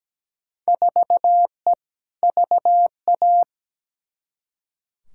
Morse code, heard as a keyed tone, about 17 wpm, 700 Hz. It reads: 4E VA
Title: Field Day Exchanges Encore - 17wpm